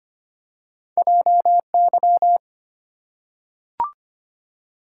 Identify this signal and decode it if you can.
Morse code, heard as a keyed tone, about 25 wpm, 700 Hz. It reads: JY
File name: Two Letter-Number Combinations - 25wpm